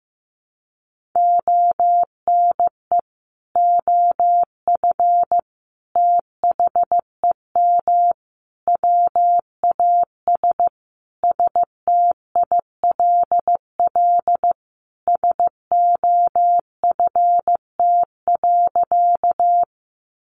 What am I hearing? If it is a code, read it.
Morse code, heard as a keyed tone, about 15 wpm, 700 Hz. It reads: ONE OF THEM WAS STILL SOFT.
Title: Sentences from Top 1500 Words 15wpm